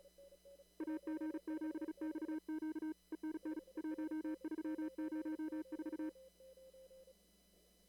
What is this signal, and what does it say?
Morse code, heard as a keyed tone, about 36 wpm, 310 Hz. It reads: AG7XQ ENN 1304
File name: CW / kt5j wpx cw-op-k5pi